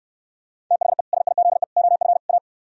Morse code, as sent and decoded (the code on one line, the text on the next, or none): - .... / ... .. --.. / --- ..-. / .-
THE SIZE OF A